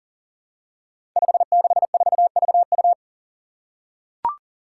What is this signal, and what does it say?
Morse code, heard as a keyed tone, about 37 wpm, 700 Hz. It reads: 564VU
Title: Five Challenging Letter-Number Combinations 40wpm